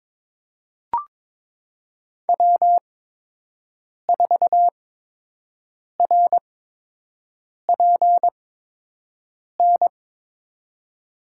Morse code, heard as a keyed tone, about 22 wpm, spaced as for 6 wpm, 700 Hz. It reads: W4RPN
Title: USA Call Signs - 5 Characters - 8x Character Spacing - 22wpm